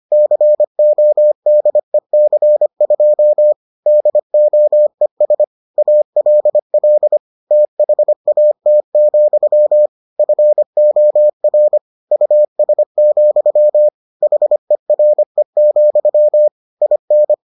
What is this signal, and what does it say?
Morse code, heard as a keyed tone, about 25 wpm, 600 Hz. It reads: CODEC2 DOES ALL THAT, FOR US, HERE, IN